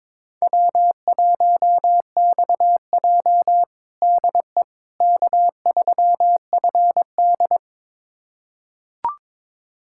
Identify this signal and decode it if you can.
Morse code, heard as a keyed tone, about 22 wpm, 700 Hz. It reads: W1XJ DE K3FD
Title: USA Callsign DE Callsign - 4 Characters - 22wpm